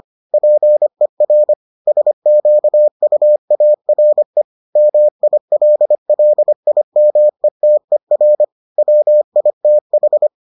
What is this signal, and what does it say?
Morse code, heard as a keyed tone, about 25 wpm, 600 Hz. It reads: PER SQUARE MILLIMETER WITH